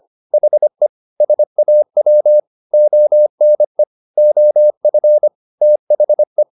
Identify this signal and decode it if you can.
Morse code, heard as a keyed tone, about 25 wpm, 600 Hz. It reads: HE SAW ONE OF THE
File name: the old man and the sea part 1 25wpm